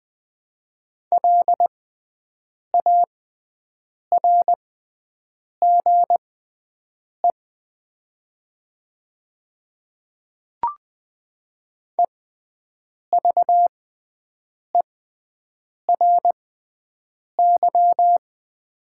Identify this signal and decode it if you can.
Morse code, heard as a keyed tone, about 20 wpm, spaced as for 7 wpm, 700 Hz. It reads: LARGE EVERY
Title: 6x Character Spacing - Top 300 Words - 20wpm